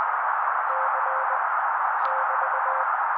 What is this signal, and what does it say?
Morse code, heard as a keyed tone, about 20 wpm, 530 Hz, with noise. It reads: C <BT>
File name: daily / 16wpm noise